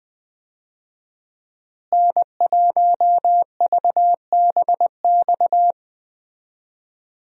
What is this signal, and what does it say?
Morse code, heard as a keyed tone, about 20 wpm, 700 Hz. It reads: N1VBX
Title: USA Call Signs - 5 Characters - Repeated 2x - 20wpm